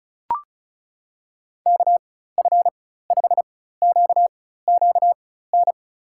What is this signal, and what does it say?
Morse code, heard as a keyed tone, about 36 wpm, spaced as for 16 wpm, 700 Hz. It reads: KF5QQN
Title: USA Call Signs - 6 Characters - 4x Character Spacing - 35wpm